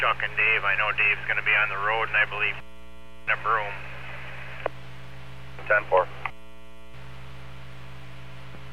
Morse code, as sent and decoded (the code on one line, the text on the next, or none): none